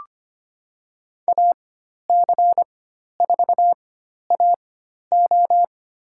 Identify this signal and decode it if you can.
Morse code, heard as a keyed tone, about 25 wpm, spaced as for 12 wpm, 700 Hz. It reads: AC4AO
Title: USA Call Signs - 5 Characters - 4x Character Spacing - 25wpm